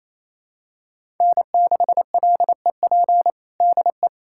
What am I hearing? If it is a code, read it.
Morse code, heard as a keyed tone, about 28 wpm, 700 Hz. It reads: N6LEP DE
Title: USA Callsign DE Callsign - 5 Characters - 28wpm